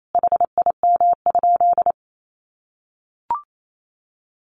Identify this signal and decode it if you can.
Morse code, heard as a keyed tone, about 28 wpm, 700 Hz. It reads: HIM?
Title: Sentences from Top 1000 Words 28wpm